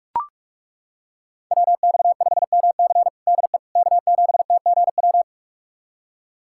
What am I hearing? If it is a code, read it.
Morse code, heard as a keyed tone, about 41 wpm, 700 Hz. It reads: WX5MC DE K7TGW
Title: USA Callsign DE Callsign - 5 Characters - 45wpm